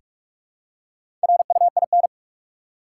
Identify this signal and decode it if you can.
Morse code, heard as a keyed tone, about 46 wpm, 700 Hz. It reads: RUIN
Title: Four Letter Words - 45wpm